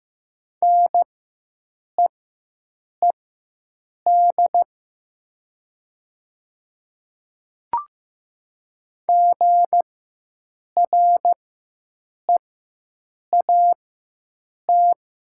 Morse code, heard as a keyed tone, about 15 wpm, spaced as for 7 wpm, 700 Hz. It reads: NEED GREAT
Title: 4x Character Spacing - Top 300 Words - 15wpm